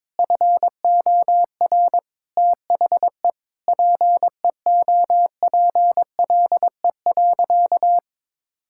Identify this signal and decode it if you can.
Morse code, heard as a keyed tone, about 21 wpm, 700 Hz. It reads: FOR THE PEOPLE.
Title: Sentences from Top 1000 Words - Review Part 5 - 22wpm